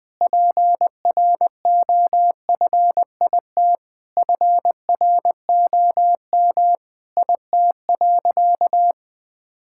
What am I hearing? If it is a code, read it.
Morse code, heard as a keyed tone, about 20 wpm, 700 Hz. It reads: PROFIT FROM IT.
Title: Sentences from Top 1000 Words - Review Part 5 - 20wpm